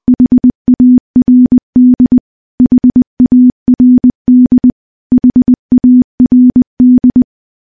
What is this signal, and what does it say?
Morse code, heard as a keyed tone, about 20 wpm, 250 Hz. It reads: HARD HARD HARD
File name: morse-training-20WPM